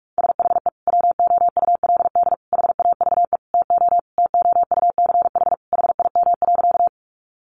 Morse code, heard as a keyed tone, about 44 wpm, 700 Hz. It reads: SHE WOULD HAVE TO TOUCH HIM.
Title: Sentences from Top 1200 Words 45wpm